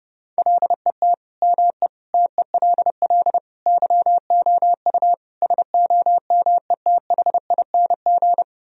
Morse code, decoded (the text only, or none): LET ME TELL YOU SOMETHING